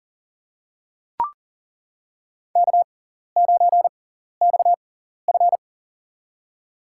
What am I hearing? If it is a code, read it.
Morse code, heard as a keyed tone, about 40 wpm, spaced as for 14 wpm, 700 Hz. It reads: K9XF
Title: USA Call Signs - 4 Characters - 6x Character Spacing - 40wpm